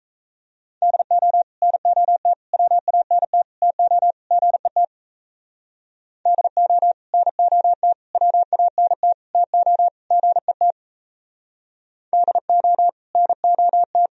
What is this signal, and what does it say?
Morse code, heard as a keyed tone, about 39 wpm, 700 Hz. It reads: DO NOT WANT TO GET DO NOT WANT TO GET DO NOT